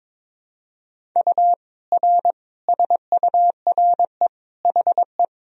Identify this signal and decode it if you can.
Morse code, heard as a keyed tone, about 21 wpm, 700 Hz. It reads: U R SURE HE
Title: Sentences from Top 300 Words and Abbreviations 22wpm